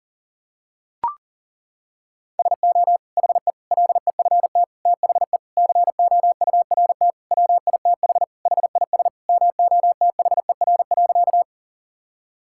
Morse code, heard as a keyed tone, about 40 wpm, 700 Hz. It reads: SO HE LEFT THE COURT WITH HIS MOTHER.
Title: Sentences from Top 1600 Words 40wpm